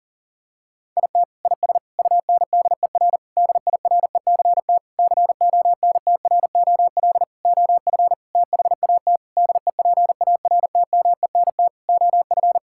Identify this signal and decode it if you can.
Morse code, heard as a keyed tone, about 40 wpm, 700 Hz. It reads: IT IS UNDER DIRECT CONTROL OF THAT DEPARTMENT OF
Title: Sentences from Top 1000 Words - Review Part 5 - 40wpm